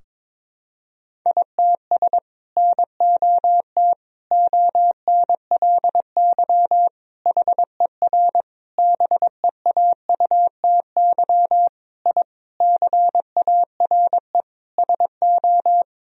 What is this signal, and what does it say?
Morse code, heard as a keyed tone, about 22 wpm, 700 Hz. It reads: ITS NOT ONLY HER BEAUTY I CARE SO
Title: Sentences from Top 1000 Words - Review Part 2 - 22wpm